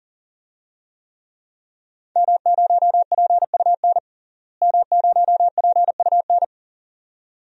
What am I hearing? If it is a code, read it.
Morse code, heard as a keyed tone, about 38 wpm, 700 Hz. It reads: M0PUN M0PUN
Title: DXCC-100 Call Signs - 5 Characters - Repeated 2x - 40wpm